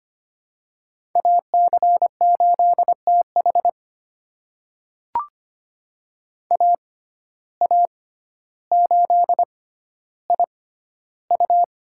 Morse code, heard as a keyed tone, about 25 wpm, spaced as for 13 wpm, 700 Hz. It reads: AC8TH AA8IU
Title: USA Call Signs - 5 Characters - 6x Character Spacing - 25wpm